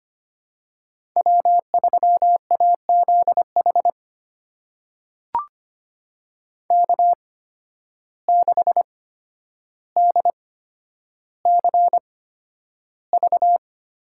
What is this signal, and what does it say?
Morse code, heard as a keyed tone, about 25 wpm, spaced as for 11 wpm, 700 Hz. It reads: W3AZH K6DCV
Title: USA Call Signs - 5 Characters - 8x Character Spacing - 25wpm